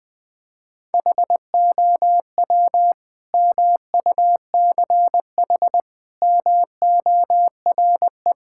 Morse code, heard as a keyed tone, about 20 wpm, 700 Hz. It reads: HOW MUCH MORE
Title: Sentences from Top 1000 Words - Review Part 5 - 20wpm